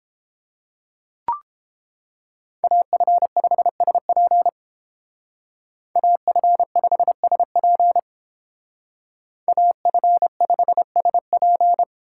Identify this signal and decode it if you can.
Morse code, heard as a keyed tone, about 30 wpm, 700 Hz. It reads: AF5SP AF5SP AF5SP